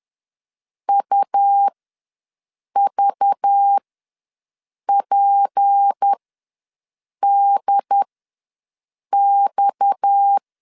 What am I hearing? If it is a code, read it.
Morse code, heard as a keyed tone, about 10 wpm, spaced as for 6 wpm, 800 Hz. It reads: UVPDX
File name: LEZIONI / 6 PKX0 / 6 PKX0 ws 10wpm 4220